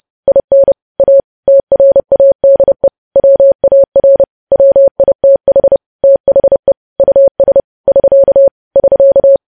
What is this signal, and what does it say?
Morse code, heard as a keyed tone, about 30 wpm, 550 Hz. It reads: IN A TRADE WAR WITH THE US <SK> <SK>
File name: podcast / News headlines-1x-30x30